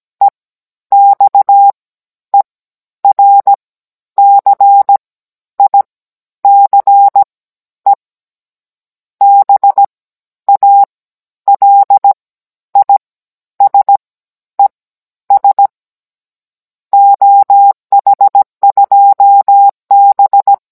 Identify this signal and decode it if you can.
Morse code, heard as a keyed tone, about 17 wpm, spaced as for 10 wpm, 800 Hz. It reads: EXERCICE BALISES OH2B
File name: Cours-NR19-16mm